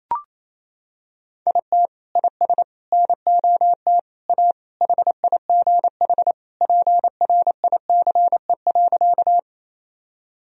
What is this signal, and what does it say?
Morse code, heard as a keyed tone, about 27 wpm, 700 Hz. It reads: IT IS NOT A HIGH PRICE.
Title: Sentences from Top 1000 Words - Review Part 4 - 28wpm